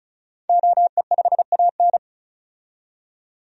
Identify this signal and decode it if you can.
Morse code, heard as a keyed tone, about 36 wpm, 700 Hz. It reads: OE5AN
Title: DXCC-100 Call Signs - 5 Characters - 35wpm